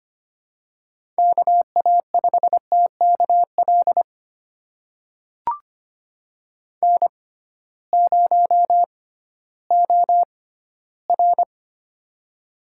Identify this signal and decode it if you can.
Morse code, heard as a keyed tone, about 25 wpm, spaced as for 15 wpm, 700 Hz. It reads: KA5TKL N0OR
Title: USA Call Signs - 6x Character Spacing - 25wpm